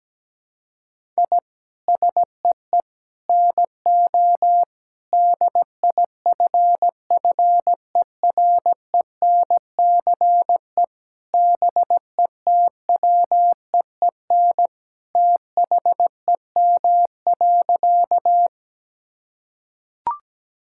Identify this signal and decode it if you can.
Morse code, heard as a keyed tone, about 17 wpm, 700 Hz. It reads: I SEE NO DIFFERENCE BETWEEN THEM.
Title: Sentences from Top 1000 Words - Review Part 5 - 17wpm